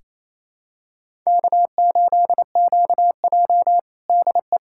Morse code, heard as a keyed tone, about 28 wpm, 700 Hz. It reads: K8QJ DE
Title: USA Callsign DE Callsign - 4 Characters - 28wpm